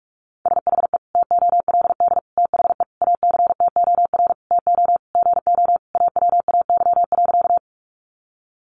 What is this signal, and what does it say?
Morse code, heard as a keyed tone, about 44 wpm, 700 Hz. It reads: SHE TOLD THE ACTOR TO GO AWAY.
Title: Sentences from Top 1500 Words 45wpm